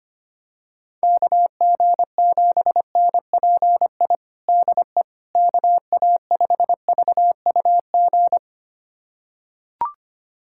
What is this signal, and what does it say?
Morse code, heard as a keyed tone, about 25 wpm, 700 Hz. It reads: KG7NPI DE KA5VUG
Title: USA Callsign DE Callsign - 6 Characters - 25wpm